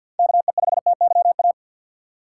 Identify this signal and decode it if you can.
Morse code, heard as a keyed tone, about 53 wpm, 700 Hz. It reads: KE5TYA
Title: USA Call Signs - 4x Character Spacing - 50wpm